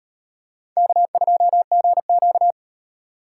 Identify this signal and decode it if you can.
Morse code, heard as a keyed tone, about 35 wpm, 700 Hz. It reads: K2GQ